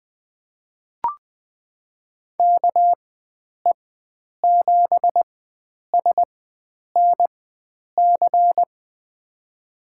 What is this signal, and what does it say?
Morse code, heard as a keyed tone, about 20 wpm, spaced as for 9 wpm, 700 Hz. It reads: KE7SNC
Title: USA Call Signs - 6 Characters - 4x Character Spacing - 20wpm